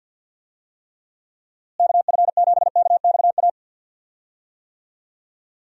Morse code, heard as a keyed tone, about 48 wpm, 700 Hz. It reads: KF7KXA
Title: USA Call Signs - 6 Characters - Repeated 3x - 50wpm